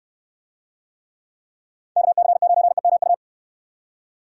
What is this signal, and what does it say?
Morse code, heard as a keyed tone, about 64 wpm, 700 Hz. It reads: KC9EMU